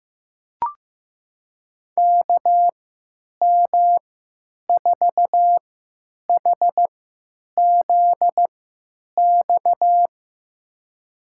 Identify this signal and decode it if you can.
Morse code, heard as a keyed tone, about 15 wpm, spaced as for 9 wpm, 700 Hz. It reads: KM4HZX
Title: USA Call Signs - 6 Characters - 3x Character Spacing - 15wpm